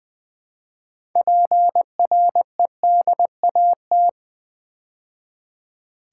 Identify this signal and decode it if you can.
Morse code, heard as a keyed tone, about 20 wpm, 700 Hz. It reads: PREDAT
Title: Build A Word - 2 Letters At A Time - 8-Letter Words - 20wpm